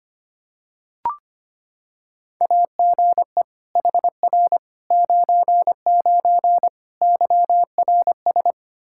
Morse code, heard as a keyed tone, about 25 wpm, 700 Hz. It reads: AGE HR 99 YRS
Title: Practice QSO - Exchange 3 - More Info - 25wpm